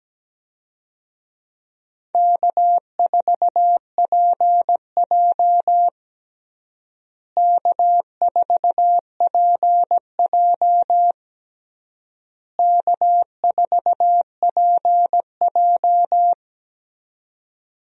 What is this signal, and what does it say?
Morse code, heard as a keyed tone, about 17 wpm, 700 Hz. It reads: K4PJ K4PJ K4PJ
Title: USA Call Signs - 4 Characters - Repeated 3x - 17wpm